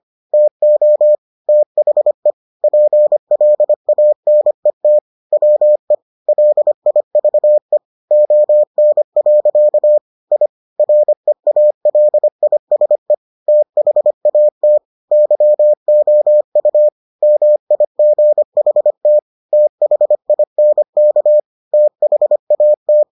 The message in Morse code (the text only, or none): TO THE PLANET WE LIVE ON. I REALISE THAT YOU MIGHT THINK THAT